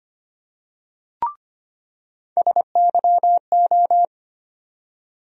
Morse code, heard as a keyed tone, about 25 wpm, 700 Hz. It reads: SYO
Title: USA Call Sign Postfixes - 3 Letters 25wpm